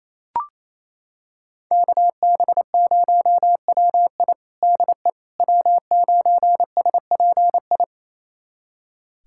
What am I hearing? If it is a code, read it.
Morse code, heard as a keyed tone, about 27 wpm, 700 Hz. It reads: KB0WI DE W9SPI
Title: USA Callsign DE Callsign - 5 Characters - 28wpm